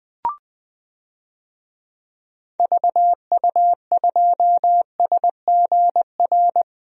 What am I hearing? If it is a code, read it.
Morse code, heard as a keyed tone, about 20 wpm, 700 Hz. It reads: VU2SGR
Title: DXCC-100 Call Signs - 6 Characters - 20wpm